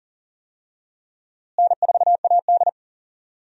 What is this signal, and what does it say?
Morse code, heard as a keyed tone, about 40 wpm, 700 Hz. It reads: N4AD
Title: USA Call Signs - 4 Characters - Repeated 3x - 40wpm